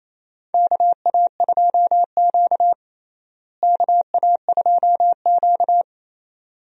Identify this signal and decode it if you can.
Morse code, heard as a keyed tone, about 28 wpm, 700 Hz. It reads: KA2Q KA2Q
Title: USA Call Signs - 4 Characters - Repeated 2x - 28wpm